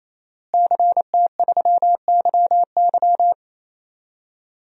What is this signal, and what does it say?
Morse code, heard as a keyed tone, about 27 wpm, 700 Hz. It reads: CT3YY
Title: DXCC-100 Call Signs - 5 Characters - Repeated 3x - 28wpm